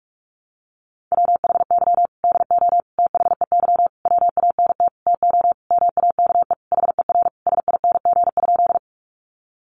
Morse code, heard as a keyed tone, about 44 wpm, 700 Hz. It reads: WHY DO THEY WANT TO MAKE HER SING?
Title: Sentences from Top 400 Words 45wpm